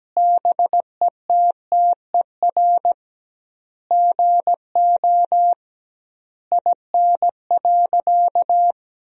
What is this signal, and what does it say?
Morse code, heard as a keyed tone, about 17 wpm, 700 Hz. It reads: BETTER GO IN.
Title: 2x Word-Spacing - Sentences from Top 400 Words - 17wpm